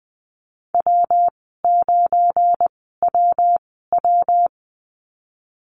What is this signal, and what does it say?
Morse code, heard as a keyed tone, about 20 wpm, spaced as for 15 wpm, 700 Hz. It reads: W9WW
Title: 173 Lesson 28g - Number 9 - Call Signs 20wpm